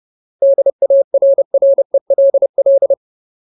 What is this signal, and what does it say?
Morse code, heard as a keyed tone, about 30 wpm, 550 Hz. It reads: DARRELL
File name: Ultimate Word List / 30WPM Ultimate